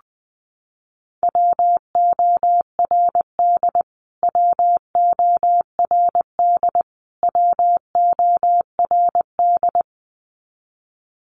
Morse code, heard as a keyed tone, about 19 wpm, 700 Hz. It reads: WORD WORD WORD
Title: Top 200 Words Repeated 20wpm